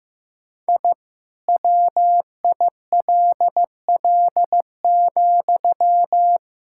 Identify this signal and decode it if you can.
Morse code, heard as a keyed tone, about 15 wpm, 700 Hz. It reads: I WILL,
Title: Sentences from Top 1000 Words - Review Part 5 - 15wpm